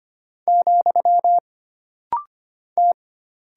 Morse code, heard as a keyed tone, about 25 wpm, 700 Hz. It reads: , T
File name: Single Character (Letter:Number:Punctuation:Prosign) - Rapid-Fire - 25wpm